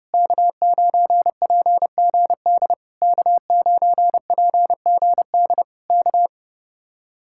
Morse code, heard as a keyed tone, about 28 wpm, 700 Hz. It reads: K9PGD K9PGD K
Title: Speed-Racing Common QSO Phrases Encore - 30wpm